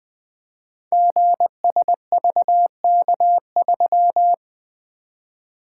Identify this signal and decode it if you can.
Morse code, heard as a keyed tone, about 20 wpm, 700 Hz. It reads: GSVK3
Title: Code Groups - 5 Letter:Numbers - 20wpm